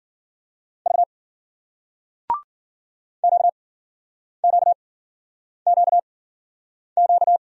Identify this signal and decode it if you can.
Morse code, heard as a keyed tone, about 50 wpm, spaced as for 9 wpm, 700 Hz. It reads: V QQQQ